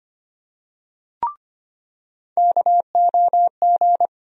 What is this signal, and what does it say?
Morse code, heard as a keyed tone, about 25 wpm, 700 Hz. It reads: KOG
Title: USA Call Sign Postfixes - 3 Letters 25wpm